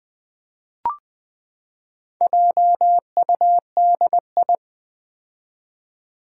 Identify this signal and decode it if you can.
Morse code, heard as a keyed tone, about 20 wpm, 700 Hz. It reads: JUDI
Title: Build A Word - 4 Letters At A Time - 8-Letter Words - 20wpm